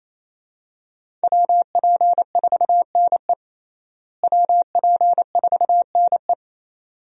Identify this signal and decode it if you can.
Morse code, heard as a keyed tone, about 28 wpm, 700 Hz. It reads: WP4NE WP4NE
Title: USA Call Signs - 5 Characters - Repeated 2x - 28wpm